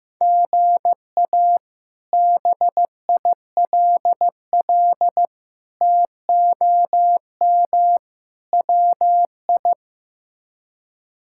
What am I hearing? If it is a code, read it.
Morse code, heard as a keyed tone, about 15 wpm, 700 Hz. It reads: GA BILL TOM WI
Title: K1USN SST QSO Practice - 15wpm